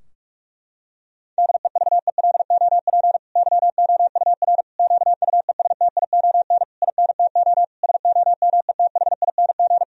Morse code, heard as a keyed tone, about 46 wpm, 700 Hz. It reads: DEVELOP YOUR QUESTION INTO SOMETHING
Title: Sentences from Top 1000 Words - Review Part 2 - 45wpm